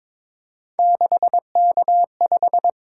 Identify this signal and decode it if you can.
Morse code, heard as a keyed tone, about 22 wpm, 700 Hz. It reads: 6K5
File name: practice / 22wpm / Three Letter-Number Combinations - 22wpm